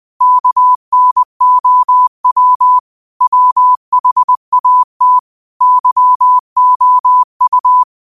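Morse code, heard as a keyed tone, about 20 wpm, 1000 Hz. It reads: KNOW WHAT YOU
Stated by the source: Morse quote practice at 20 wpm.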